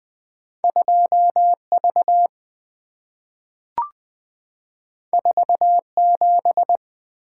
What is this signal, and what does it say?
Morse code, heard as a keyed tone, about 20 wpm, 700 Hz. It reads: 2 V 4 7
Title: Two Letter-Number Combinations - 20wpm